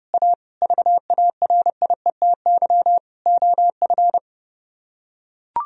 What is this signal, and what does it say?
Morse code, heard as a keyed tone, about 32 wpm, 700 Hz. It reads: A VARIETY OF